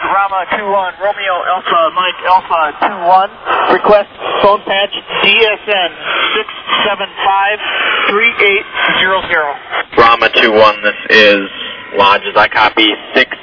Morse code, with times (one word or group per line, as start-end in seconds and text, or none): none